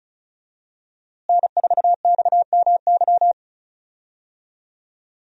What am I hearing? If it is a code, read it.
Morse code, heard as a keyed tone, about 34 wpm, 700 Hz. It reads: N4XMY
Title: USA Call Signs - 5 Characters - Repeated 3x - 35wpm